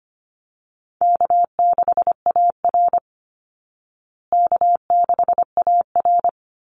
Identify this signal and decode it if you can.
Morse code, heard as a keyed tone, about 25 wpm, 700 Hz. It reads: K6AR K6AR
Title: Easy Call Signs Repeated 2x - 25wpm